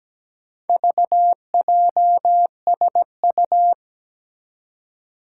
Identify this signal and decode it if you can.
Morse code, heard as a keyed tone, about 17 wpm, 700 Hz. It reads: VJSU
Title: Four Challenging Letter-Number Combinations 17wpm